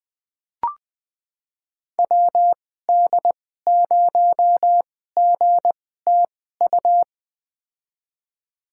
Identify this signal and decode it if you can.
Morse code, heard as a keyed tone, about 20 wpm, spaced as for 14 wpm, 700 Hz. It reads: WD0GTU
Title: USA Call Signs - 6 Characters - 2x Character Spacing - 20wpm